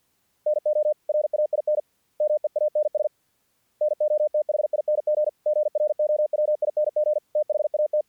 Morse code, heard as a keyed tone, about 48 wpm, 600 Hz. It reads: NO RAIN MEANS NOTHING GROWING THAT
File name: archived / CRSnet